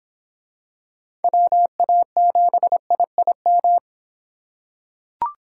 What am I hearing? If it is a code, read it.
Morse code, heard as a keyed tone, about 24 wpm, 700 Hz. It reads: WA7IIM